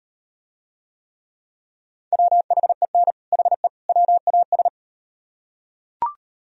Morse code, heard as a keyed tone, about 36 wpm, 700 Hz. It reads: WHEN HE WAS